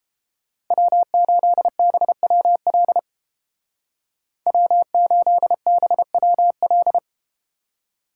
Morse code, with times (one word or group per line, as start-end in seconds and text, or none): 0.7-3.0 W8BWL
4.5-7.0 W8BWL